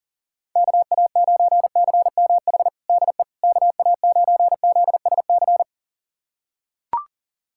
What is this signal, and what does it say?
Morse code, heard as a keyed tone, about 40 wpm, 700 Hz. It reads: KA9CMH DE KA9ZSC
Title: USA Callsign DE Callsign 40wpm